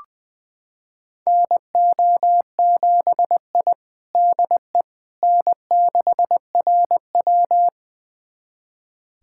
Morse code, heard as a keyed tone, about 21 wpm, 700 Hz. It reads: NO7I DE N6RW
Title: USA Callsign DE Callsign - 4 Characters - 20wpm